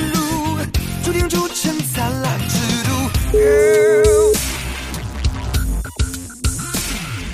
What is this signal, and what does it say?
Ringback tone (Europe, 425 Hz), with noise.